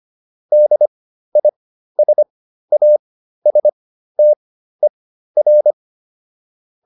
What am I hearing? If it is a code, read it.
Morse code, heard as a keyed tone, about 25 wpm, spaced as for 13 wpm, 600 Hz. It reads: DISASTER